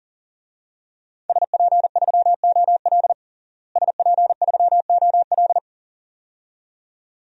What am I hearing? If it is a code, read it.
Morse code, heard as a keyed tone, about 40 wpm, 700 Hz. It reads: SP3OL SP3OL
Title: DXCC-100 Call Signs - 5 Characters - Repeated 2x - 40wpm